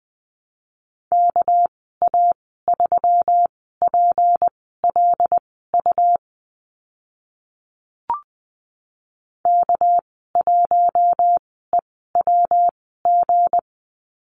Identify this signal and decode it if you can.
Morse code, heard as a keyed tone, about 20 wpm, spaced as for 15 wpm, 700 Hz. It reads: KA3PLU K1EWG
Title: 200 Lesson 32g - Letter k - Call Signs 20wpm